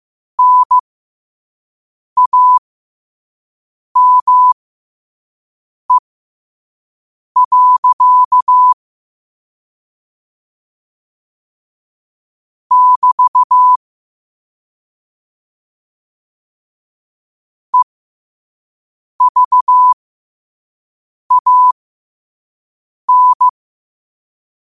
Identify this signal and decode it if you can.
Morse code, heard as a keyed tone, about 15 wpm, spaced as for 5 wpm, 1000 Hz. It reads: NAME. <BT> EVAN
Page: New quotes every day in morse code at 5 Words per minute.